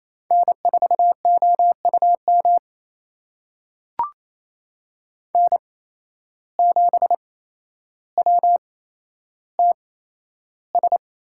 Morse code, heard as a keyed tone, about 27 wpm, spaced as for 12 wpm, 700 Hz. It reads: N4OUM N7WTS
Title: USA Call Signs - 5 Characters - 8x Character Spacing - 28wpm